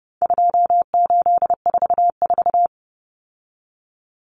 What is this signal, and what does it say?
Morse code, heard as a keyed tone, about 30 wpm, 700 Hz. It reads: 2844
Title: Numbers - 4-Digit 30wpm